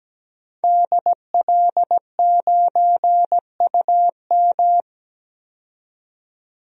Morse code, heard as a keyed tone, about 17 wpm, 700 Hz. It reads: DL9UM
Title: DXCC-100 Call Signs - 5 Characters - 17wpm